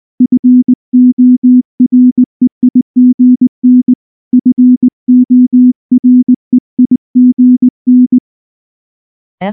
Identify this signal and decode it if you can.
Morse code, heard as a keyed tone, about 19 wpm, 250 Hz. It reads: FOREIGN FOREIGN
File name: morse / morse-training-20WPM-6